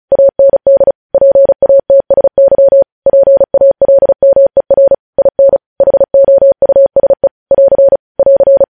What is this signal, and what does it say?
Morse code, heard as a keyed tone, about 36 wpm, 550 Hz. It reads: AND PATSY PALMER IN HOUSE <AR> <AR>
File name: podcast / News headlines-1x-35x35